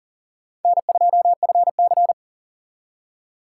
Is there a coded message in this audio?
Yes: Morse code from 0.7 to 2.1 s, heard as a keyed tone, about 51 wpm, 700 Hz. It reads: N2FC